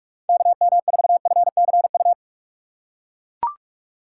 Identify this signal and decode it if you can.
Morse code, heard as a keyed tone, about 41 wpm, 700 Hz. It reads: KM4FCU
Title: USA Call Signs - 6 Characters - 2x Character Spacing - 45wpm